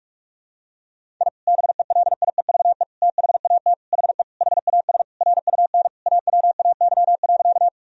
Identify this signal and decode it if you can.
Morse code, heard as a keyed tone, about 41 wpm, 700 Hz. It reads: I BELIEVE THAT HE HAS RUN AWAY.
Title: Sentences from Top 1000 Words - Review Part 1 - 45wpm